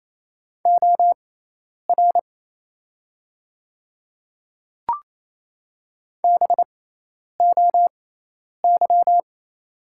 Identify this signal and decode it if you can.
Morse code, heard as a keyed tone, about 28 wpm, spaced as for 10 wpm, 700 Hz. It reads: OR BOY
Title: 6x Character Spacing - Top 200 Words - 28wpm